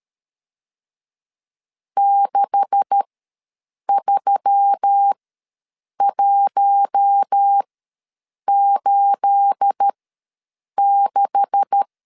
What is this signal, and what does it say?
Morse code, heard as a keyed tone, about 13 wpm, spaced as for 7 wpm, 800 Hz. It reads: 63186
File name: LEZIONI / 9 286 ws 12wpm 3351